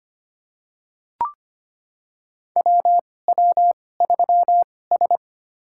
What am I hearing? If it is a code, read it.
Morse code, heard as a keyed tone, about 25 wpm, spaced as for 18 wpm, 700 Hz. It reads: WW3S
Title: USA Call Signs - 4 Characters - 2x Character Spacing - 25wpm